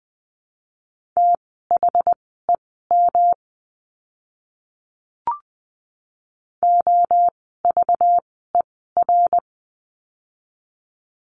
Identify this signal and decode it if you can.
Morse code, heard as a keyed tone, about 20 wpm, spaced as for 14 wpm, 700 Hz. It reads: THEM OVER
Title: 206 Lesson 33f - Letter j - Words 20wpm